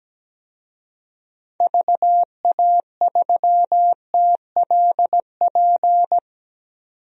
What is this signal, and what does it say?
Morse code, heard as a keyed tone, about 17 wpm, 700 Hz. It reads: VA3TLP
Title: DXCC-100 Call Signs - 6 Characters - 17wpm